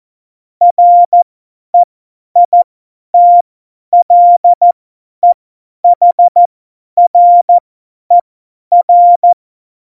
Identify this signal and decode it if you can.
Morse code, heard as a keyed tone, about 14 wpm, spaced as for 10 wpm, 700 Hz. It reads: REITLEHRER
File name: Morsekurs / Witze / Witz 12 Mai 2020, 50 F70